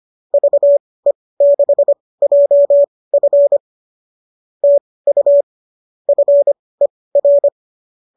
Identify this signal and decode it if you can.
Morse code, heard as a keyed tone, about 25 wpm, spaced as for 18 wpm, 590 Hz. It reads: VE6JF TU FER
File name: qso203 18